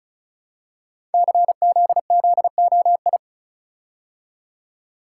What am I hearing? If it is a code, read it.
Morse code, heard as a keyed tone, about 36 wpm, 700 Hz. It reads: CZZOI
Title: Code Groups - 5 Letters - 35wpm